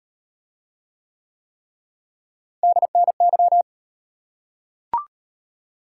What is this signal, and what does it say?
Morse code, heard as a keyed tone, about 36 wpm, 700 Hz. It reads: DNY